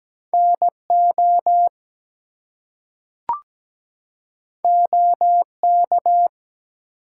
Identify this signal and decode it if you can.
Morse code, heard as a keyed tone, about 17 wpm, 700 Hz. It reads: NO OK
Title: Two Letter Words - 17wpm